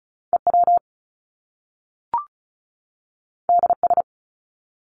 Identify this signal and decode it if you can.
Morse code, heard as a keyed tone, about 36 wpm, 700 Hz. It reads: EW DS
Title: Top Two Letter Combinations in Words 35wpm